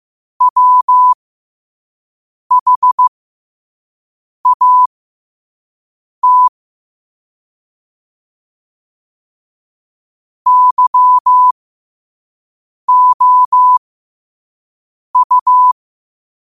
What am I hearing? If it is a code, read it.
Morse code, heard as a keyed tone, about 15 wpm, spaced as for 5 wpm, 1000 Hz. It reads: WHAT YOU
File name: qotdfiles / 5wpm